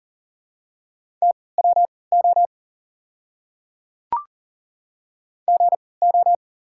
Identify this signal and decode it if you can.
Morse code, heard as a keyed tone, about 40 wpm, spaced as for 23 wpm, 690 Hz. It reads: TWO GO
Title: 3x Character Spacing - Top 100 Words - 40wpm